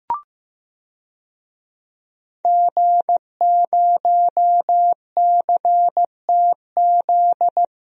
Morse code, heard as a keyed tone, about 15 wpm, 700 Hz. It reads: G0CTZ
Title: DXCC-100 Call Signs - 5 Characters - 15wpm